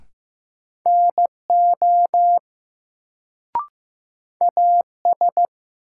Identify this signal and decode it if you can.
Morse code, heard as a keyed tone, about 15 wpm, 700 Hz. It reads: NO AS
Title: Sets of 2 Words - New Word Drill - 15wpm